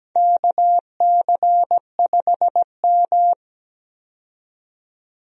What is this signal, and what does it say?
Morse code, heard as a keyed tone, about 17 wpm, 700 Hz. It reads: KC5M
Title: USA Call Signs - 4 Characters - 17wpm